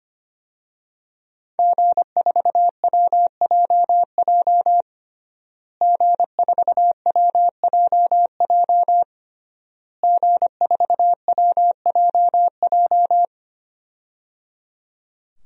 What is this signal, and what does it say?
Morse code, heard as a keyed tone, about 25 wpm, 700 Hz. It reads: G4WJJ G4WJJ G4WJJ
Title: DXCC-100 Call Signs - 5 Characters - Repeated 3x - 25wpm